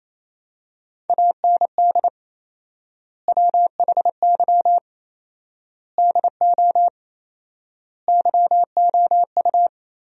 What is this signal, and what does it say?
Morse code, heard as a keyed tone, about 28 wpm, 700 Hz. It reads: AND WHY DO YOU
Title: 4x Word-Spacing - Sentences from Top 400 Words - 28wpm